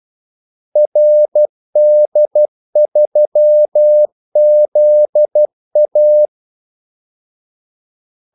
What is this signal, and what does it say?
Morse code, heard as a keyed tone, about 12 wpm, 600 Hz. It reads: RD3ZA
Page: Kaldesignaler 12 wpm